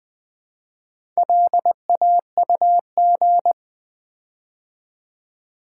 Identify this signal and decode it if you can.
Morse code, heard as a keyed tone, about 20 wpm, 700 Hz. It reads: LAUG
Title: Build A Word - 4 Letters At A Time - 8-Letter Words - 20wpm